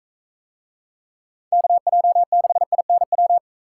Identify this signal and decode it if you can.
Morse code, heard as a keyed tone, about 41 wpm, 700 Hz. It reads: KJ6INW